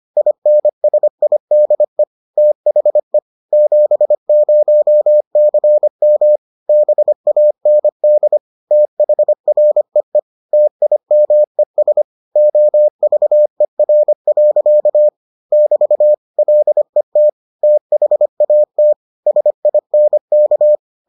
Morse code, heard as a keyed tone, about 25 wpm, 600 Hz. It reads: INSIDE THE 70CM BAND THREE TIMES OVER. <BT> LET THAT SINK